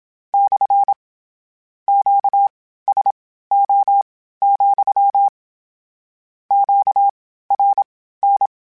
Morse code, heard as a keyed tone, about 27 wpm, spaced as for 15 wpm, 800 Hz. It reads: / QSO, QRN
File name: Q Calls / Q Calls-mp3s / Q Calls SPSL2 @25